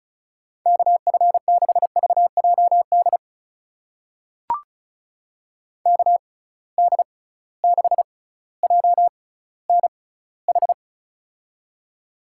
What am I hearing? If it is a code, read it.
Morse code, heard as a keyed tone, about 36 wpm, spaced as for 18 wpm, 700 Hz. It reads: KF6VJD KD6JNH
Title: USA Call Signs - 6 Characters - 6x Character Spacing - 35wpm